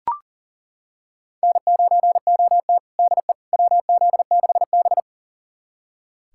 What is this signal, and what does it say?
Morse code, heard as a keyed tone, about 37 wpm, 700 Hz. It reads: N9OT DE WZ6B
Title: USA Callsign DE Callsign - 4 Characters - 40wpm